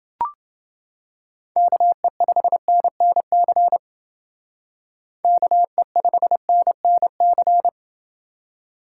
Morse code, heard as a keyed tone, about 30 wpm, 700 Hz. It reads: KE5NNC KE5NNC